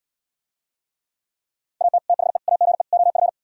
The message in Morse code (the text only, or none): AT THE AGE OF